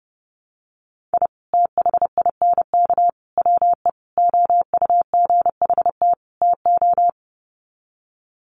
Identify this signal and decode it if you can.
Morse code, heard as a keyed tone, about 30 wpm, 700 Hz. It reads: I THINK WE OUGHT TO
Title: Sets of 5 Words Encore 30wpm